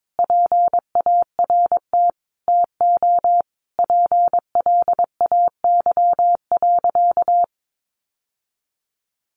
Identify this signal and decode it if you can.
Morse code, heard as a keyed tone, about 22 wpm, 700 Hz. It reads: PART TO PLAY.
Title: Sentences from Top 900 Words 22wpm